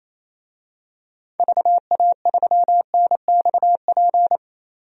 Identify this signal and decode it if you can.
Morse code, heard as a keyed tone, about 28 wpm, 700 Hz. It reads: VA3NXP
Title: DXCC-100 Call Signs - 6 Characters - Repeated 2x - 28wpm